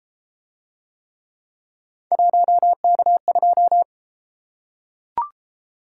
Morse code, heard as a keyed tone, about 33 wpm, 700 Hz. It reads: 1K2